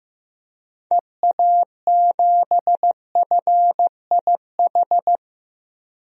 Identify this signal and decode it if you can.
Morse code, heard as a keyed tone, about 15 wpm, 700 Hz. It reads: EA7FIH
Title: DXCC-100 Call Signs - 6 Characters - 15wpm